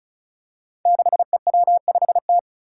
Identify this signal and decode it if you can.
Morse code, heard as a keyed tone, about 36 wpm, 700 Hz. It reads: 6EW5T
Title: Five Challenging Letter-Number Combinations 35wpm